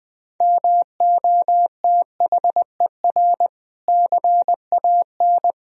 Morse code, heard as a keyed tone, about 21 wpm, 700 Hz. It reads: MOTHER CAN
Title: Sentences from Top 1000 Words - Review Part 4 - 20wpm